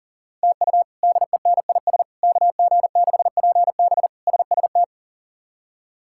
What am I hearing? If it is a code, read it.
Morse code, heard as a keyed tone, about 40 wpm, 700 Hz. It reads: TU DENIS KG6PB SST
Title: K1USN SST QSO Practice - 40wpm